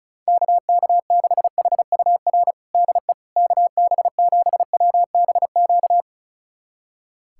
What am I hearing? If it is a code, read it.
Morse code, heard as a keyed tone, about 36 wpm, 700 Hz. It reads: KK6HUR DE KB7WBQ
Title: USA Callsign DE Callsign - 6 Characters - 35wpm